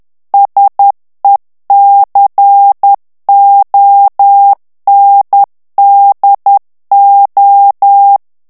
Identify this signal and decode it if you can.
Morse code, heard as a keyed tone, about 10 wpm, 800 Hz. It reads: SECONDO
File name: ITALIEN texte clair 2 vitesse 10 mots minute